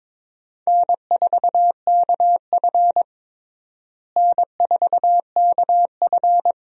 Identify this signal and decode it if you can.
Morse code, heard as a keyed tone, about 22 wpm, 700 Hz. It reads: N4KF N4KF
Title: USA Call Signs - 4 Characters - Repeated 2x - 22wpm